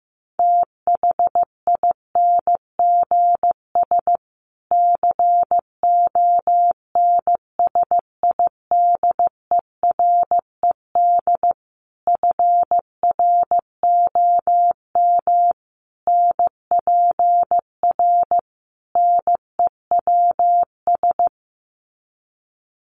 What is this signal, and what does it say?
Morse code, heard as a keyed tone, about 15 wpm, 700 Hz. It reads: THINGS CONSIDERED FROM NPR NEWS
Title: Sets of 5 Words Encore 15wpm